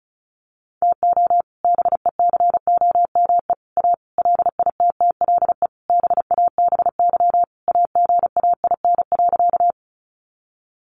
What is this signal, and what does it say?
Morse code, heard as a keyed tone, about 34 wpm, 700 Hz. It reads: TO BECOME A LITTLE BABY AGAIN.
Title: Sentences from Top 1300 Words 35wpm